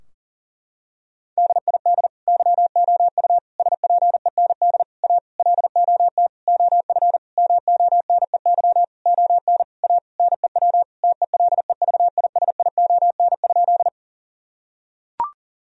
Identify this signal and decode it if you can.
Morse code, heard as a keyed tone, about 40 wpm, 700 Hz. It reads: DID YOU SPEND A LOT OF MONEY ON A NEW TELEVISION?
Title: Sentences from Top 1000 Words - Review Part 4 - 40wpm